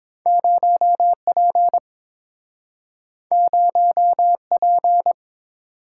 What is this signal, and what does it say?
Morse code, heard as a keyed tone, about 30 wpm, 700 Hz. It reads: 0P 0P